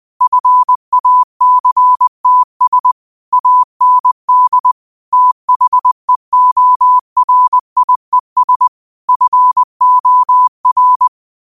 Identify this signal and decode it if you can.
Morse code, heard as a keyed tone, about 20 wpm, 1000 Hz. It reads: FACTS AND THEORIES FOR
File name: qotdfiles / 20wpm